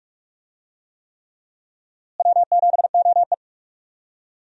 Morse code, heard as a keyed tone, about 46 wpm, 700 Hz. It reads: W7OE
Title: USA Call Signs - 4 Characters - Repeated 3x - 45wpm